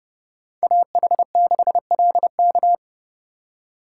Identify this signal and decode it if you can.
Morse code, heard as a keyed tone, about 35 wpm, 700 Hz. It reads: AH6LK
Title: Speed Racing - USA 2x2 Call Signs 30wpm